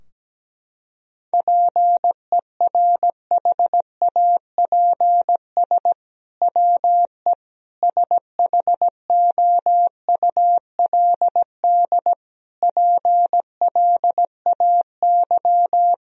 Morse code, heard as a keyed tone, about 17 wpm, 700 Hz. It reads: PERHAPS WE SHOULD PLAY
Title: Sentences from Top 1000 Words - Review Part 3 - 17wpm